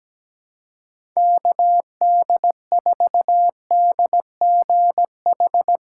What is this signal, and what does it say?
Morse code, heard as a keyed tone, about 17 wpm, 700 Hz. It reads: KD4DGH
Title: USA Call Signs - 6 Characters - 6x Character Spacing - 17wpm